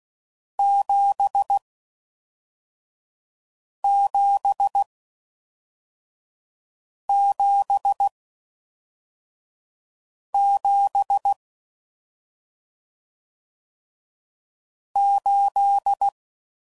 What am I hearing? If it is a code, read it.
Morse code, heard as a keyed tone, about 16 wpm, 800 Hz. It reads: 7 7 7 7 8